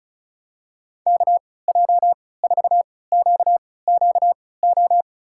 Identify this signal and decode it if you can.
Morse code, heard as a keyed tone, about 36 wpm, spaced as for 20 wpm, 700 Hz. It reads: KJ4QQO
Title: USA Call Signs - 6 Characters - 3x Character Spacing - 35wpm